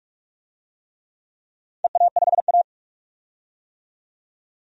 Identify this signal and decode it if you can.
Morse code, heard as a keyed tone, about 44 wpm, 700 Hz. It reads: EA5A
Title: DXCC-100 Call Signs - 4 Characters - 45wpm